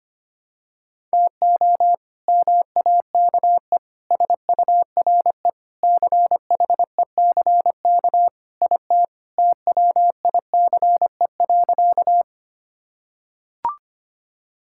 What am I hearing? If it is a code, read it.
Morse code, heard as a keyed tone, about 25 wpm, 700 Hz. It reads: TO MAKE SURE CHECK IT TWICE.
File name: practice / 25wpm / Sentences with 3-Word Phrases - 25wpm